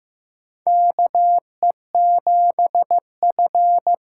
Morse code, heard as a keyed tone, about 15 wpm, 700 Hz. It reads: KE7F
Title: USA Call Signs - 6 Characters - 6x Character Spacing - 15wpm